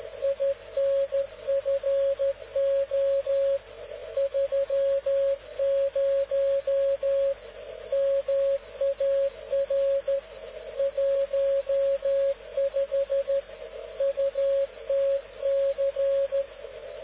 Morse code, heard as a keyed tone, about 13 wpm, 550 Hz. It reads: INFO 30 MAR 15 UTC